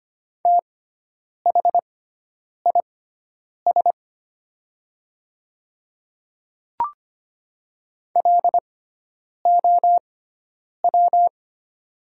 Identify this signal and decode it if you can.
Morse code, heard as a keyed tone, about 25 wpm, spaced as for 9 wpm, 700 Hz. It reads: THIS LOW
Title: 6x Character Spacing - Top 300 Words - 25wpm